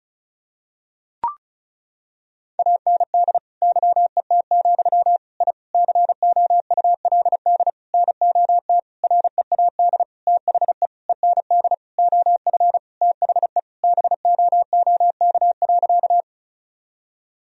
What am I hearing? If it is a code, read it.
Morse code, heard as a keyed tone, about 34 wpm, 700 Hz. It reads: AND YET, I COULD NOT READ THE END OF THE BOOK.
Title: Sentences from Top 1000 Words - Review Part 1 - 35wpm